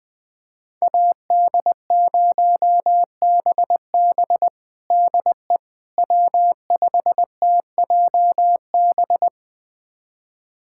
Morse code, heard as a keyed tone, about 20 wpm, 700 Hz. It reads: AD0BB DE W5TJB
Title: USA Callsign DE Callsign - 5 Characters - 20wpm